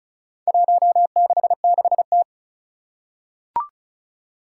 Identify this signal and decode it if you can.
Morse code, heard as a keyed tone, about 36 wpm, 700 Hz. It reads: 166T
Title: Four Challenging Letter-Number Combinations 35wpm